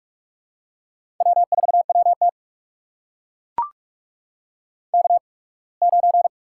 Morse code, heard as a keyed tone, about 44 wpm, spaced as for 27 wpm, 700 Hz. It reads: W4WT K9
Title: USA Call Signs - 4 Characters - 8x Character Spacing - 45wpm